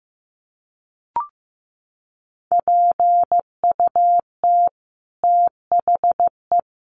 Morse code, heard as a keyed tone, about 15 wpm, 700 Hz. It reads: PUT THE
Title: Sentences from Top 1200 Words 15wpm